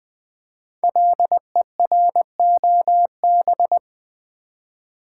Morse code, heard as a keyed tone, about 20 wpm, 700 Hz. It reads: LEROB
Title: Code Groups - 5 Letters - 20wpm